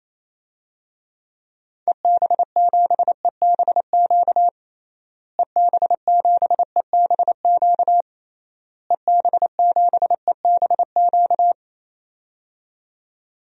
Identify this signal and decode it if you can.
Morse code, heard as a keyed tone, about 28 wpm, 700 Hz. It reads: EB7EBQ EB7EBQ EB7EBQ
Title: DXCC-100 Call Signs - 6 Characters - Repeated 3x - 28wpm